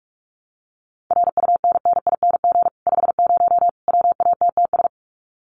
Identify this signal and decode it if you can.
Morse code, heard as a keyed tone, about 44 wpm, 700 Hz. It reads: RUNNING 50 WATTS
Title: Common QSO Phrases 45wpm